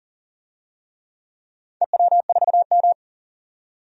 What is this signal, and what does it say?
Morse code, heard as a keyed tone, about 40 wpm, 700 Hz. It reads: EW4M
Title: DXCC-100 Call Signs - 4 Characters - 40wpm